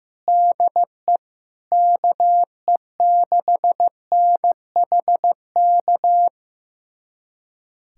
Morse code, heard as a keyed tone, about 16 wpm, 700 Hz. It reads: DE KE6NHK
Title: USA Callsign DE Callsign 15wpm